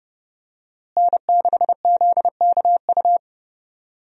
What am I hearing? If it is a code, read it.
Morse code, heard as a keyed tone, about 31 wpm, 700 Hz. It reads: N6ZKU
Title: USA Call Signs - 5 Characters - Repeated 2x - 30wpm